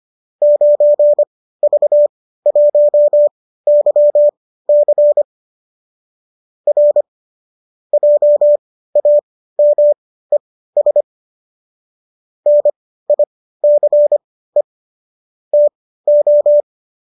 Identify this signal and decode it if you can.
Morse code, heard as a keyed tone, about 25 wpm, spaced as for 15 wpm, 590 Hz. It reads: 9V1YC R JAMES NICE TO